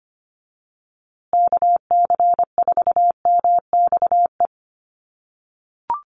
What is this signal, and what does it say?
Morse code, heard as a keyed tone, about 25 wpm, 700 Hz. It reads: KC4MXE
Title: USA Call Signs Repeated 3x 25wpm